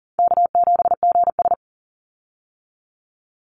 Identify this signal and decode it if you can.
Morse code, heard as a keyed tone, about 40 wpm, 700 Hz. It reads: K7GS
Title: Easy Call Signs Repeated 2x - 40wpm